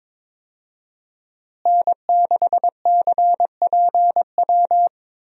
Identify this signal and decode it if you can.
Morse code, heard as a keyed tone, about 22 wpm, 700 Hz. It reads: N6CPW
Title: USA Call Signs - 5 Characters - Repeated 2x - 22wpm